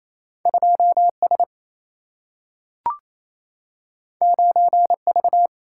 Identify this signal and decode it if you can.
Morse code, heard as a keyed tone, about 28 wpm, 700 Hz. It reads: 2 S 9 V
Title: Single-Letter Single-Number 28wpm